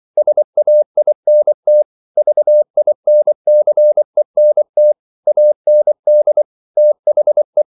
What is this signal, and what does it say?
Morse code, heard as a keyed tone, about 24 wpm, 600 Hz. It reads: SAINT VINCENT AND THE
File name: CW / Practice / NationalCapitals24